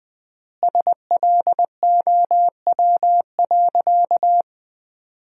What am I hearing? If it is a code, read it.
Morse code, heard as a keyed tone, about 20 wpm, 700 Hz. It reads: SLOW.
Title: 3x Word-Spacing - Sentences from Top 400 Words - 20wpm